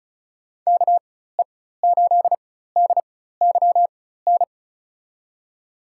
Morse code, heard as a keyed tone, about 36 wpm, spaced as for 16 wpm, 700 Hz. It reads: KE8DYN
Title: USA Call Signs - 4x Character Spacing - 35wpm